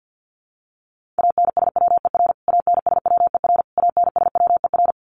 Morse code, heard as a keyed tone, about 50 wpm, 700 Hz. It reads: ANSWER ANSWER ANSWER
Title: Top 500 Words Repeated 50wpm